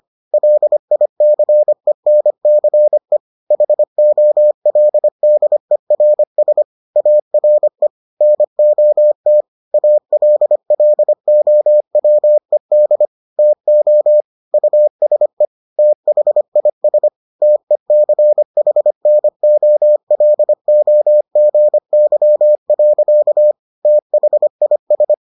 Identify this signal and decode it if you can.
Morse code, heard as a keyed tone, about 25 wpm, 600 Hz. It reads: LICENCE HOLDERS ARE NOT ALLOWED TO USE THIS TECHNOLOGY. THIS